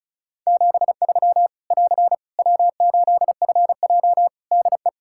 Morse code, heard as a keyed tone, about 36 wpm, 700 Hz. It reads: 73 <AR> W8FJ DE
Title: Practice QSO - Exchange 4 - Ending - 35wpm